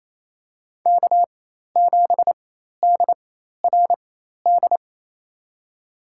Morse code, heard as a keyed tone, about 28 wpm, spaced as for 13 wpm, 700 Hz. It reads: K7DRD
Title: USA Call Signs - 5 Characters - 4x Character Spacing - 28wpm